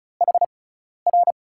Morse code, heard as a keyed tone, about 36 wpm, spaced as for 12 wpm, 700 Hz. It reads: HR
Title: USA Call Signs - 6 Characters - 6x Character Spacing - 35wpm